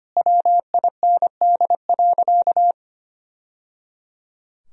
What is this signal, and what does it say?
Morse code, heard as a keyed tone, about 25 wpm, 700 Hz. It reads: WIND.
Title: Sentences from Top 1000 Words - Review Part 3 - 25wpm